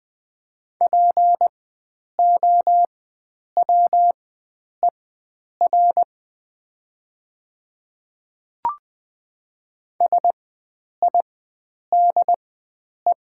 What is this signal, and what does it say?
Morse code, heard as a keyed tone, about 20 wpm, spaced as for 9 wpm, 700 Hz. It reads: POWER SIDE
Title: 4x Character Spacing - Top 500 Words - 20wpm